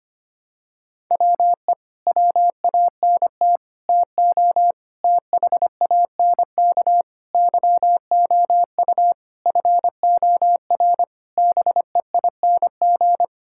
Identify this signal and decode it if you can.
Morse code, heard as a keyed tone, about 25 wpm, 700 Hz. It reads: WE WANT TO THANK YOU FOR BEING
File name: Sets of 9 Words - 25wpm